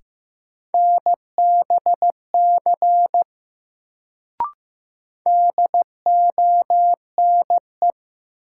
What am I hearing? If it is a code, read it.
Morse code, heard as a keyed tone, about 15 wpm, 700 Hz. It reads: NBC DONE
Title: Sets of 5 Words Encore - New Word Drill - 15wpm